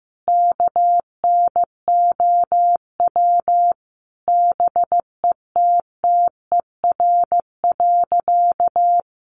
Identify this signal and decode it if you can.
Morse code, heard as a keyed tone, about 15 wpm, 700 Hz. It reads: KNOW BETTER.
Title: Sentences from Top 1400 Words 15wpm